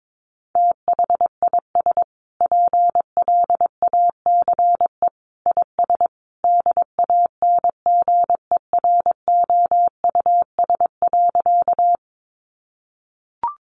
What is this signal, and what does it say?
Morse code, heard as a keyed tone, about 22 wpm, 700 Hz. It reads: THIS PLACE IS DANGEROUS.
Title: Sentences from Top 1200 Words 22wpm